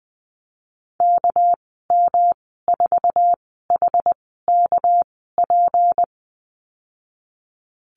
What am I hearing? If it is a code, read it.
Morse code, heard as a keyed tone, about 20 wpm, spaced as for 15 wpm, 700 Hz. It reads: KM4HKP
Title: 200 Lesson 32g - Letter k - Call Signs 20wpm